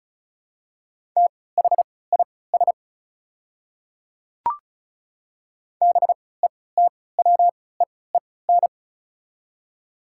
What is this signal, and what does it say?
Morse code, heard as a keyed tone, about 36 wpm, spaced as for 20 wpm, 690 Hz. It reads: THIS BETWEEN
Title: 3x Character Spacing - Top 300 Words - 35wpm